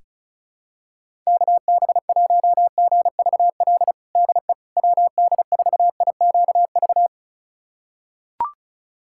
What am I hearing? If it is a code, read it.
Morse code, heard as a keyed tone, about 34 wpm, 700 Hz. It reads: KB1GVL DE WD4IQV
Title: USA Callsign DE Callsign - 6 Characters - 35wpm